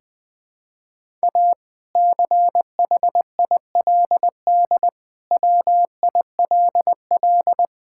Morse code, heard as a keyed tone, about 21 wpm, 700 Hz. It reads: A CHILD WILL
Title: Sentences from Top 1000 Words - Review Part 6 - 20wpm